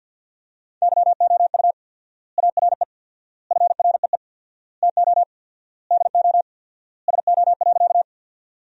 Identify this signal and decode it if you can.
Morse code, heard as a keyed tone, about 50 wpm, 700 Hz. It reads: YOU ARE FREE TO DO SO.
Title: 4x Word-Spacing - Sentences from Top 500 Words - 50wpm